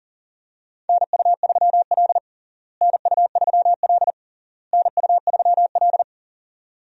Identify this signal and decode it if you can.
Morse code, heard as a keyed tone, about 37 wpm, 700 Hz. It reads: NU3L NU3L NU3L
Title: USA Call Signs - 4 Characters - Repeated 3x - 40wpm